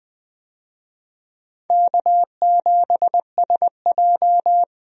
Morse code, heard as a keyed tone, about 20 wpm, 700 Hz. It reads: K7SJ
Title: USA Call Signs - 4 Characters - 20wpm